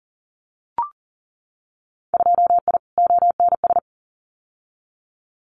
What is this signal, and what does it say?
Morse code, heard as a keyed tone, about 40 wpm, 700 Hz. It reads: 2I ONS
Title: Winter Field Day Exchanges Encore 40wpm